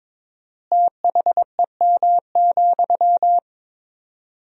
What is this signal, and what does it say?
Morse code, heard as a keyed tone, about 22 wpm, 700 Hz. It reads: THEM,
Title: 4x Word-Spacing - Sentences from Top 400 Words - 22wpm